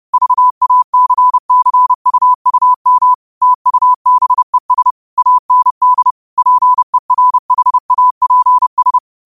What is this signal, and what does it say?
Morse code, heard as a keyed tone, about 30 wpm, 1000 Hz. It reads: VACCUUM TUBES AND PERHAPS
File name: qotdfiles / 30wpm